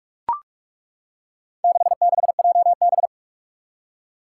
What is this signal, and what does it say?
Morse code, heard as a keyed tone, about 41 wpm, 700 Hz. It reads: 66JB
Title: Four Challenging Letter-Number Combinations 45wpm